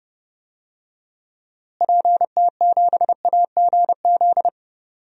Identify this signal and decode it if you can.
Morse code, heard as a keyed tone, about 31 wpm, 700 Hz. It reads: PT7AGZ
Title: DXCC-100 Call Signs - 6 Characters - Repeated 3x - 30wpm